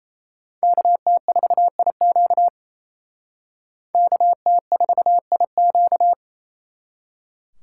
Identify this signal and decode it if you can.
Morse code, heard as a keyed tone, about 36 wpm, 700 Hz. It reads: KT4IQ KT4IQ